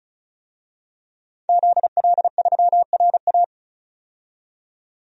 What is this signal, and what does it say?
Morse code, heard as a keyed tone, about 34 wpm, 700 Hz. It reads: ZL3RA
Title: DXCC-100 Call Signs - 5 Characters - 35wpm